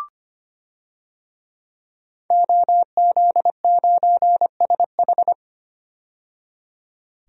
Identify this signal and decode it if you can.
Morse code, heard as a keyed tone, about 25 wpm, 700 Hz. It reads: OZ9SH
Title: DXCC-100 Call Signs - 5 Characters - 25wpm